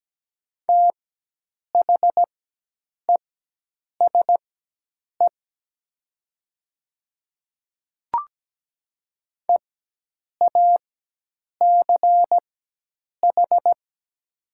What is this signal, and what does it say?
Morse code, heard as a keyed tone, about 17 wpm, spaced as for 8 wpm, 700 Hz. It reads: THESE EACH
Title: 4x Character Spacing - Top 100 Words - 17wpm